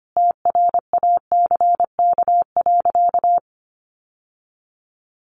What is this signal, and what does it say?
Morse code, heard as a keyed tone, about 25 wpm, 690 Hz. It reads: TRACK.
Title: Sentences from Top 1100 Words 25wpm